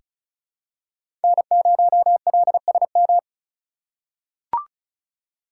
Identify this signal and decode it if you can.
Morse code, heard as a keyed tone, about 36 wpm, 700 Hz. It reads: N0LSM
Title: USA Call Signs - 3x Character Spacing - 35wpm